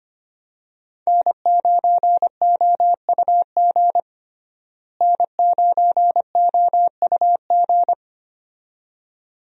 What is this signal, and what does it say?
Morse code, heard as a keyed tone, about 25 wpm, 700 Hz. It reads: N9OUG N9OUG
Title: USA Call Signs - 5 Characters - Repeated 2x - 25wpm